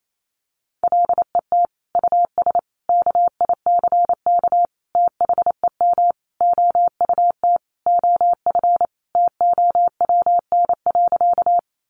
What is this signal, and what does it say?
Morse code, heard as a keyed tone, about 28 wpm, 700 Hz. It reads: LET US KICK THEM OUT OF TOWN.
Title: Sentences from Top 1300 Words 28wpm